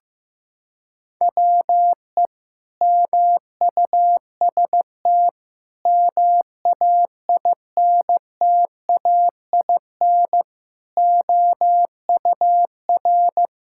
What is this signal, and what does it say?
Morse code, heard as a keyed tone, about 15 wpm, 700 Hz. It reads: WE MUST MAINTAIN OUR
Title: Sentences from Top 1000 Words - Review Part 5 - 15wpm